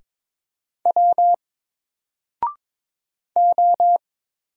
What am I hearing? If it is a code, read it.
Morse code, heard as a keyed tone, about 22 wpm, 700 Hz. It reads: W O
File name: Single-Letter-Number-22wpm